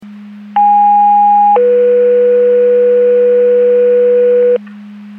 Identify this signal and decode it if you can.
Two-tone page, heard as a one-second tone, then a three-second tone, about 820 then 490 Hz, with noise. It is heard from 0.6 to 4.6 s.
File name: Portage 154.385 Paging